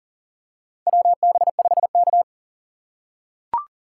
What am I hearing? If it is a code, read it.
Morse code, heard as a keyed tone, about 40 wpm, 700 Hz. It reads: WB5K
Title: K1USN SST QSO Practice - 40wpm